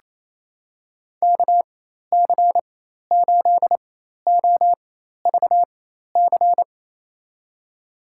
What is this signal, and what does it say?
Morse code, heard as a keyed tone, about 28 wpm, spaced as for 13 wpm, 700 Hz. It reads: KC8OVC
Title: USA Call Signs - 6 Characters - 4x Character Spacing - 28wpm